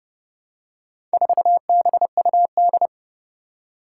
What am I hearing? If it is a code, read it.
Morse code, heard as a keyed tone, about 30 wpm, 700 Hz. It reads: 4BUD
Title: Four Challenging Letter-Number Combinations 30wpm